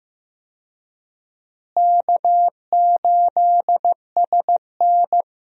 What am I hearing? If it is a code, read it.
Morse code, heard as a keyed tone, about 15 wpm, 700 Hz. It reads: K8SN
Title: USA Call Signs - 4 Characters - Repeated 2x - 15wpm